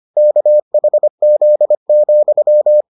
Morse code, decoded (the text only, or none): KHZ,